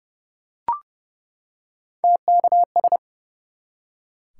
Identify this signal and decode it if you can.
Morse code, heard as a keyed tone, about 30 wpm, 700 Hz. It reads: TKS
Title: Top 180 QSO Elements 30wpm